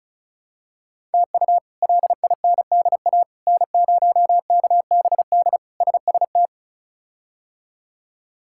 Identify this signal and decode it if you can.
Morse code, heard as a keyed tone, about 36 wpm, 700 Hz. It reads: TU LINDA N0KBD SST
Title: K1USN SST QSO Practice - 35wpm